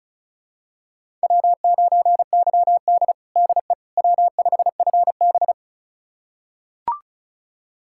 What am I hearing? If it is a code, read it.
Morse code, heard as a keyed tone, about 36 wpm, 700 Hz. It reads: W9YD DE W5FB
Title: USA Callsign DE Callsign - 4 Characters - 35wpm